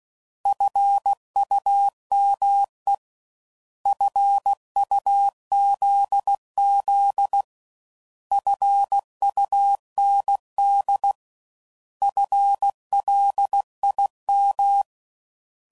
Morse code, heard as a keyed tone, about 16 wpm, 800 Hz. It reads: FUME FUZZ FUND FLIM